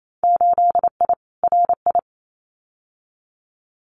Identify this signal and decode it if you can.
Morse code, heard as a keyed tone, about 28 wpm, 700 Hz. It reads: 8I RI
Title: Winter Field Day Exchanges Encore 28wpm